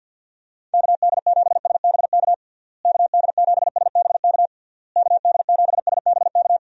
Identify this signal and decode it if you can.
Morse code, heard as a keyed tone, about 49 wpm, 700 Hz. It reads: KD7SBK KD7SBK KD7SBK
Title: USA Call Signs - 6 Characters - Repeated 3x - 50wpm